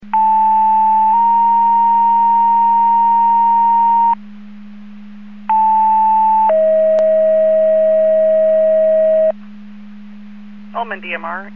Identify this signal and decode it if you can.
Two-tone page, heard as a one-second tone, then a three-second tone, about 880 then 930 Hz, about 880 then 640 Hz, with speech. It is heard from 0.1 to 9.3 s.